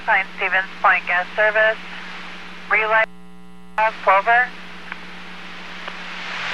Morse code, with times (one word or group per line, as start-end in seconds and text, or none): none